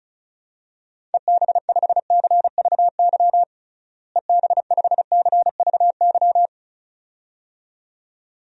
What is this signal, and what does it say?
Morse code, heard as a keyed tone, about 36 wpm, 700 Hz. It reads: EB5CVY EB5CVY
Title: DXCC-100 Call Signs - 6 Characters - Repeated 2x - 35wpm